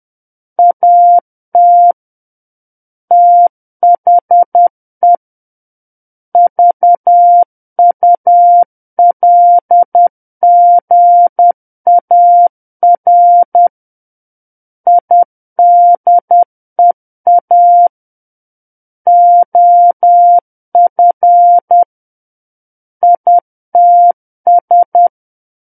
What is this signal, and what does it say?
Morse code, heard as a keyed tone, about 10 wpm, 700 Hz. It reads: AT THE VULGAR IDEA OF ITS
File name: Book One / 01-Chapter 1 - 10 WPM